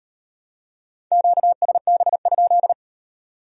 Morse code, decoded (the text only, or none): QSB?